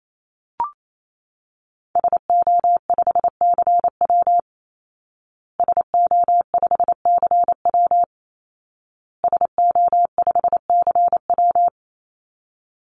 Morse code, heard as a keyed tone, about 28 wpm, 700 Hz. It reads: SO5CW SO5CW SO5CW
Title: Call Signs Repeated 2x - 28wpm